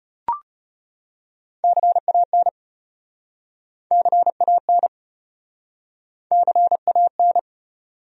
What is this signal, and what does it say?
Morse code, heard as a keyed tone, about 33 wpm, 700 Hz. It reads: CAN CAN CAN